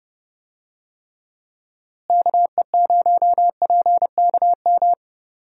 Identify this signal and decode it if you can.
Morse code, heard as a keyed tone, about 30 wpm, 700 Hz. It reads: KE0PKM
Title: USA Call Signs - 6 Characters - Repeated 3x - 30wpm